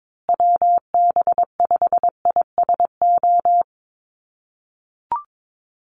Morse code, heard as a keyed tone, about 22 wpm, 700 Hz. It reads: WB5ISO
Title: USA Call Signs Repeated 3x 22wpm